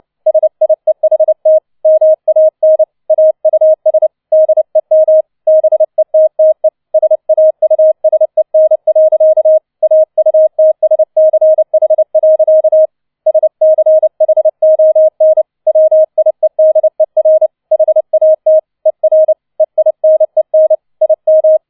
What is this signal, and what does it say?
Morse code, heard as a keyed tone, about 29 wpm, 630 Hz. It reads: SIEHT MAN AUS DEM BETTE SAUSEN. AUTSCH. SCHON WIEDER HAT ER EINEN IM